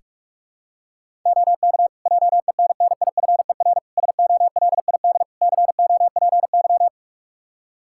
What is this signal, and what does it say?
Morse code, heard as a keyed tone, about 46 wpm, 700 Hz. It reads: OK JENNIFER SOLID COPY
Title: Practice QSO - Exchange 3 - More Info - 45wpm